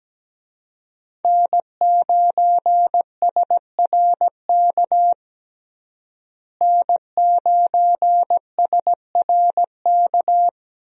Morse code, heard as a keyed tone, about 17 wpm, 700 Hz. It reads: N9SRK N9SRK
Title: USA Call Signs - 5 Characters - Repeated 2x - 17wpm